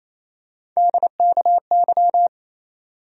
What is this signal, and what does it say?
Morse code, heard as a keyed tone, about 28 wpm, 700 Hz. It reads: DKY
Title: USA Call Sign Postfixes - 3 Letters 28wpm